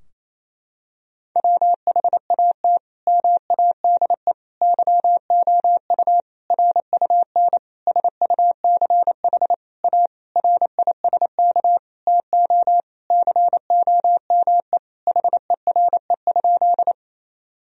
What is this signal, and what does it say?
Morse code, heard as a keyed tone, about 27 wpm, 700 Hz. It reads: WHAT MADE YOU RUN SUCH A RISK TO COME HERE?
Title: Sentences from Top 1000 Words - Review Part 4 - 28wpm